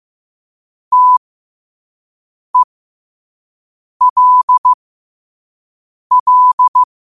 Morse code, heard as a keyed tone, about 15 wpm, spaced as for 5 wpm, 1000 Hz. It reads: TELL
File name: qotdfiles / 5wpm